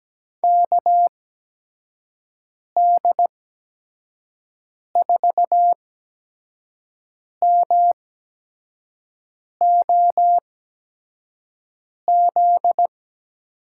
Morse code, heard as a keyed tone, about 17 wpm, spaced as for 5 wpm, 700 Hz. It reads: KD4MOZ
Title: USA Call Signs - 8x Character Spacing - 17wpm